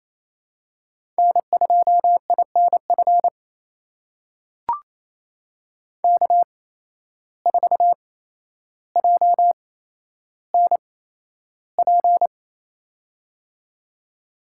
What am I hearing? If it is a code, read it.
Morse code, heard as a keyed tone, about 28 wpm, spaced as for 12 wpm, 700 Hz. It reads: N2INF K4JNP
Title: USA Call Signs - 5 Characters - 8x Character Spacing - 28wpm